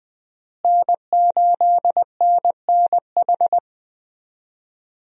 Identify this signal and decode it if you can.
Morse code, heard as a keyed tone, about 20 wpm, 700 Hz. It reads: N8NNH
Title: USA Call Signs - 5 Characters - Repeated 2x - 20wpm